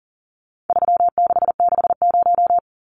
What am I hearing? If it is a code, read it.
Morse code, heard as a keyed tone, about 40 wpm, 700 Hz. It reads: 3660
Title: Numbers - 4-Digit 40wpm